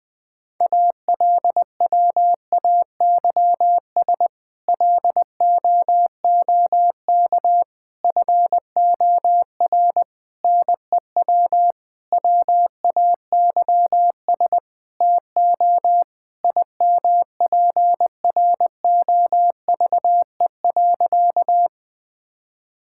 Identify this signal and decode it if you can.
Morse code, heard as a keyed tone, about 19 wpm, 700 Hz. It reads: ALWAYS LOOK FOR NEW WAYS TO IMPROVE.
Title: Sentences with 6-Word Phrases - Spoken Beginning - 20wpm